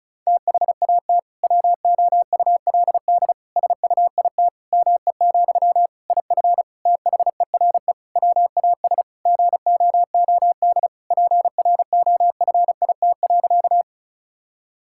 Morse code, heard as a keyed tone, about 36 wpm, 700 Hz. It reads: THAT WOULD SUIT ME, IF THERE WAS GOOD PROFIT.
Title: Sentences from Top 1600 Words 35wpm